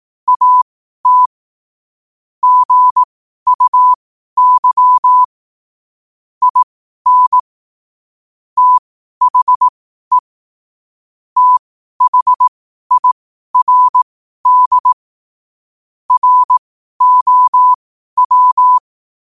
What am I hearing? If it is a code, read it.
Morse code, heard as a keyed tone, about 18 wpm, spaced as for 12 wpm, 1000 Hz. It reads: AT GUY IN THE THIRD ROW
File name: qotdfiles / 12wpm